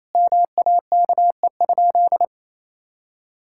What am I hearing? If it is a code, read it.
Morse code, heard as a keyed tone, about 27 wpm, 700 Hz. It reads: MAKE?
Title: Sentences from Top 1000 Words - Review Part 5 - 28wpm